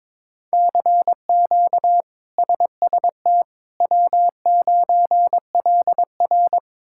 Morse code, heard as a keyed tone, about 21 wpm, 700 Hz. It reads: CQ SST W9LR
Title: K1USN SST QSO Practice - 22wpm